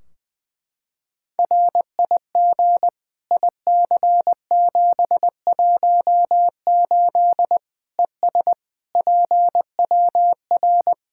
Morse code, heard as a keyed tone, about 20 wpm, 700 Hz. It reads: RIG IC718 ES PWR
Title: Practice QSO - Exchange 2 - Info - 20wpm